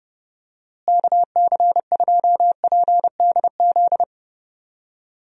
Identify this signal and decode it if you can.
Morse code, heard as a keyed tone, about 31 wpm, 700 Hz. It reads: KC2PDZ
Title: USA Call Signs - 6 Characters - Repeated 3x - 30wpm